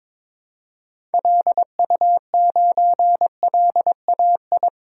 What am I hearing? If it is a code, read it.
Morse code, heard as a keyed tone, about 22 wpm, 700 Hz. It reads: LU9LAI
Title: DXCC-100 Call Signs - 6 Characters - 22wpm